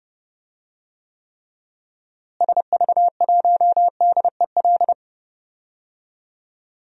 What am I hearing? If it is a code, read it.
Morse code, heard as a keyed tone, about 32 wpm, 700 Hz. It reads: SV1DEL
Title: DXCC-100 Call Signs - 6 Characters - Repeated 2x - 30wpm